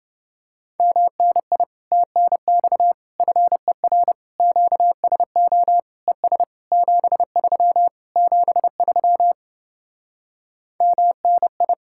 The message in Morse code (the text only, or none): MNI TNX FER QSO ES 73 73 MNI